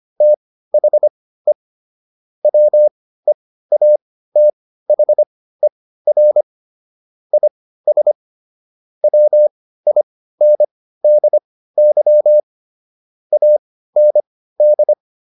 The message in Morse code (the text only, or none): THE WEATHER IS WINDY AND